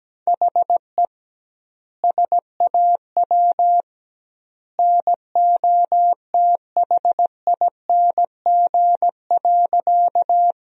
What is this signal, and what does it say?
Morse code, heard as a keyed tone, about 17 wpm, 700 Hz. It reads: HE SAW NOTHING.
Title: 2x Word-Spacing - Sentences from Top 500 Words - 17wpm